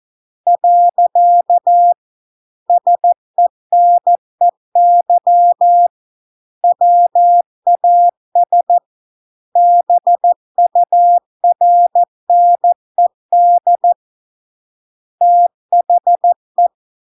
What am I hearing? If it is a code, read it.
Morse code, heard as a keyed tone, about 14 wpm, 700 Hz. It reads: . SENEY WAS BURNED THE